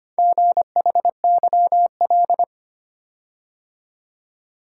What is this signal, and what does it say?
Morse code, heard as a keyed tone, about 25 wpm, 700 Hz. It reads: GHYL
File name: practice / 25wpm / Four Letter-Number Combinations - 25wpm